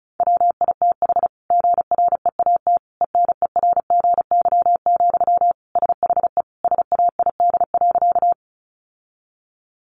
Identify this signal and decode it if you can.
Morse code, heard as a keyed tone, about 34 wpm, 700 Hz. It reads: WITH GREAT ENERGY, SHE SAID.
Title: Sentences from Top 900 Words 35wpm